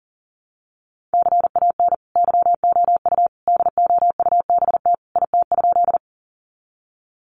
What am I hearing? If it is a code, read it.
Morse code, heard as a keyed tone, about 40 wpm, 700 Hz. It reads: CAN YOU DOUBT IT?
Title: Sentences from Top 1500 Words 40wpm